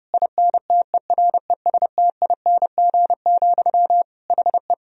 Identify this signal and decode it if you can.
Morse code, heard as a keyed tone, about 32 wpm, 700 Hz. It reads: INTERESTING, HE
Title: Sentences from Top 1000 Words - Review Part 3 - 30wpm